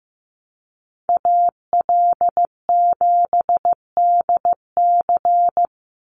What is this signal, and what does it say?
Morse code, heard as a keyed tone, about 15 wpm, 700 Hz. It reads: AL7DC
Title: USA Call Signs 15wpm